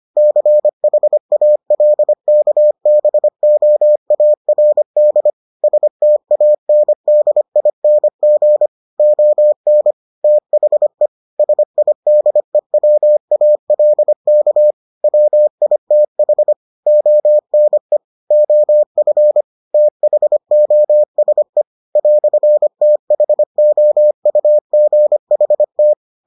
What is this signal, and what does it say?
Morse code, heard as a keyed tone, about 25 wpm, 600 Hz. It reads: CHALKBOARD STANDING ON THE SIDEWALK WITH ONE OF THOSE "THOUGHT